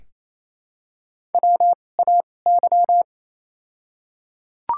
Morse code, heard as a keyed tone, about 28 wpm, spaced as for 20 wpm, 700 Hz. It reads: WAY
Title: 2x Character Spacing - Top 500 Words - 28wpm